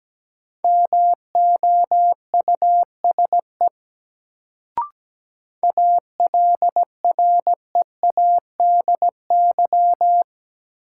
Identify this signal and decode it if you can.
Morse code, heard as a keyed tone, about 17 wpm, 700 Hz. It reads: MOUSE ALREADY
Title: Sentences from Top 700 Words - New Word Drill - 17wpm